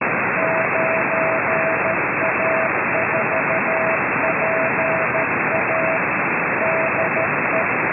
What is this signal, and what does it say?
Morse code, heard as a keyed tone, about 13 wpm, 650 Hz. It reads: 9A4PA DE